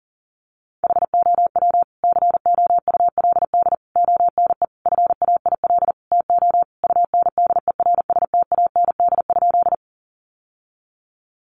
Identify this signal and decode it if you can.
Morse code, heard as a keyed tone, about 40 wpm, 700 Hz. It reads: HOW COULD ONE FAIL TO UNDERSTAND?
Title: Sentences from Top 1100 Words 40wpm